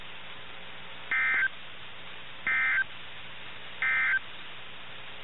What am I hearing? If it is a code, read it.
Emergency Alert System (SAME) tones, heard as harsh buzzing bursts. It is heard from 1.0 to 4.3 s.
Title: NOAA Weather Radio - KIH61